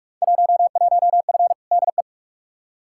Morse code, heard as a keyed tone, about 47 wpm, 700 Hz. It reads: 11F DE
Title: Field Day Exchanges Encore - 45wpm